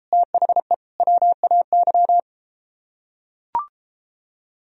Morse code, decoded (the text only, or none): THE WAY